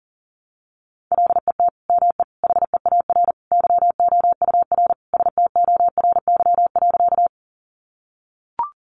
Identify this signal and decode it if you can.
Morse code, heard as a keyed tone, about 40 wpm, 700 Hz. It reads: LET ME HEAR YOUR STORY.
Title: Sentences from Top 300 Words 40wpm